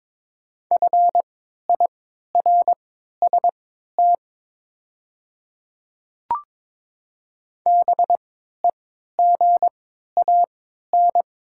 Morse code, heard as a keyed tone, about 23 wpm, spaced as for 13 wpm, 700 Hz. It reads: FIRST BEGAN
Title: 3x Character Spacing - Top 300 Words - 22wpm